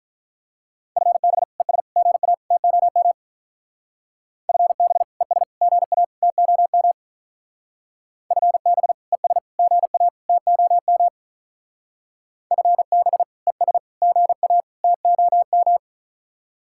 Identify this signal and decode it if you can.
Morse code, heard as a keyed tone, about 44 wpm, 700 Hz. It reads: FB ES GA TOM FB ES GA TOM FB ES GA TOM FB ES GA TOM